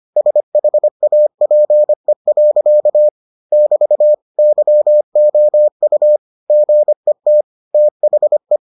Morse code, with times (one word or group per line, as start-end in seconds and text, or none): 0.2-3.1 SHAPE.
3.5-4.1 <BT>
4.4-6.2 YOU
6.5-7.4 GET
7.7-8.6 THE